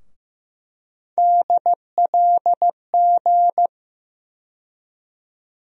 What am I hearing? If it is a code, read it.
Morse code, heard as a keyed tone, about 15 wpm, 700 Hz. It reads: DLG
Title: USA Call Sign Postfixes - 3 Letters 15wpm